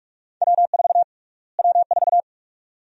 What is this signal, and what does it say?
Morse code, heard as a keyed tone, about 46 wpm, 700 Hz. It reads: W4 W4
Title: USA Call Sign Prefixes - Repeated 2x - 45wpm